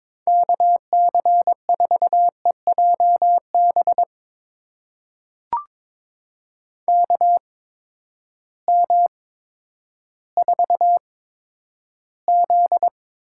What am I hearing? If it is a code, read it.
Morse code, heard as a keyed tone, about 22 wpm, spaced as for 11 wpm, 700 Hz. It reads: KC4EJB KM4Z
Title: USA Call Signs - 6 Characters - 8x Character Spacing - 22wpm